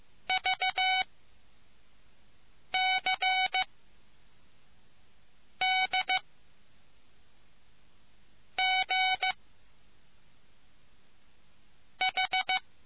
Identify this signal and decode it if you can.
Morse code, heard as a keyed tone, about 15 wpm, 750 Hz. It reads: V C D G H